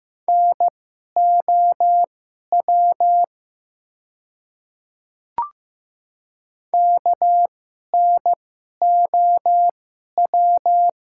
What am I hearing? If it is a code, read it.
Morse code, heard as a keyed tone, about 15 wpm, spaced as for 11 wpm, 700 Hz. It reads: NOW KNOW
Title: 2x Character Spacing - Top 100 Words - 15wpm